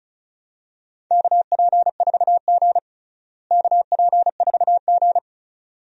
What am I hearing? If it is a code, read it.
Morse code, heard as a keyed tone, about 36 wpm, 700 Hz. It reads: KP4G KP4G
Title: DXCC-100 Call Signs - 4 Characters - Repeated 2x - 35wpm